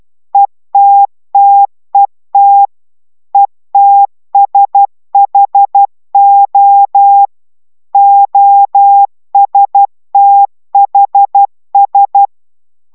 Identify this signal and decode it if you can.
Morse code, heard as a keyed tone, about 12 wpm, 800 Hz. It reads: ETTET ETSHO OSTHS